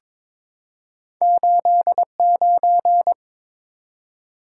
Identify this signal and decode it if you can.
Morse code, heard as a keyed tone, about 22 wpm, 700 Hz. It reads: 89
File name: practice / 22wpm / Two Letter-Number Combinations - 22wpm